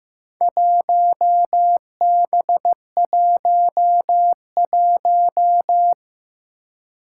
Morse code, heard as a keyed tone, about 15 wpm, 700 Hz. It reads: 1B11
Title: Four Challenging Letter-Number Combinations 15wpm